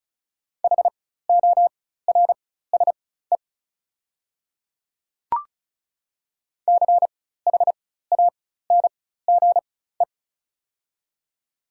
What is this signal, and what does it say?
Morse code, heard as a keyed tone, about 35 wpm, spaced as for 16 wpm, 690 Hz. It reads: HORSE CHANGE
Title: 4x Character Spacing - Top 500 Words - 35wpm